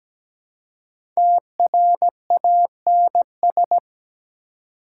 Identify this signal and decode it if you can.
Morse code, heard as a keyed tone, about 21 wpm, 700 Hz. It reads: TRANS
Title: Speed-Racing Prefixes and Suffixes 17wpm